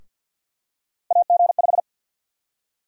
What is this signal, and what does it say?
Morse code, heard as a keyed tone, about 50 wpm, 700 Hz. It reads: AG5
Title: USA Call Sign Prefixes - 50wpm